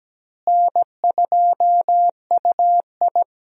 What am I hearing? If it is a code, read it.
Morse code, heard as a keyed tone, about 17 wpm, 700 Hz. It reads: N2UI
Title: USA Call Signs - 4 Characters - Repeated 3x - 17wpm